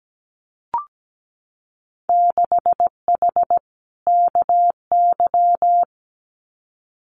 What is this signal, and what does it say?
Morse code, heard as a keyed tone, about 17 wpm, 700 Hz. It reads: 6H KY
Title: Winter Field Day Exchanges Encore 17wpm